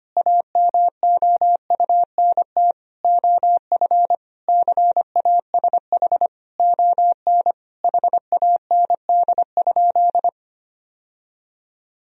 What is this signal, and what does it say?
Morse code, heard as a keyed tone, about 25 wpm, 700 Hz. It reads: AMOUNT OF CASH ON HAND?
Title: Sentences from Top 1000 Words - Review Part 4 - 25wpm